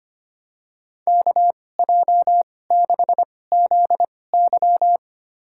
Morse code, heard as a keyed tone, about 25 wpm, spaced as for 18 wpm, 700 Hz. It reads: KJ6ZY
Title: USA Call Signs - 5 Characters - 2x Character Spacing - 25wpm